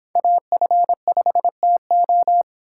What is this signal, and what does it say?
Morse code, heard as a keyed tone, about 26 wpm, 700 Hz. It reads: AF5TO